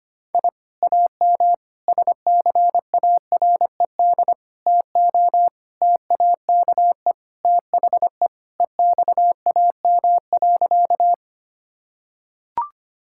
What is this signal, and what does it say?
Morse code, heard as a keyed tone, about 25 wpm, 700 Hz. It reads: I AM SCARED TO TAKE THE EXAM.
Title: Sentences from Top 1000 Words - Review Part 5 - 25wpm